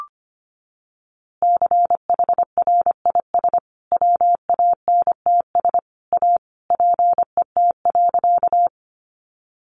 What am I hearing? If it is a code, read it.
Morse code, heard as a keyed tone, about 25 wpm, 700 Hz. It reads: CHRIS WANTS A PET.
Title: Easy Phrases 25wpm